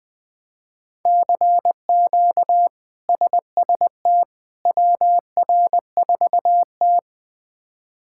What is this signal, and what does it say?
Morse code, heard as a keyed tone, about 20 wpm, 700 Hz. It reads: CQ SST WR4T
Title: K1USN SST QSO Practice - 20wpm